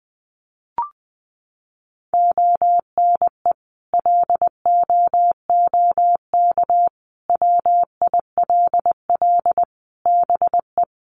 Morse code, heard as a keyed tone, about 20 wpm, 700 Hz. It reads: ONE LOOK WILL BE
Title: Sentences from Top 1400 Words 20wpm